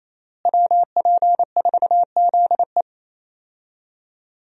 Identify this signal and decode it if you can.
Morse code, heard as a keyed tone, about 28 wpm, 700 Hz. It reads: WP4ZE
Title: USA Call Signs - 5 Characters - Repeated 3x - 28wpm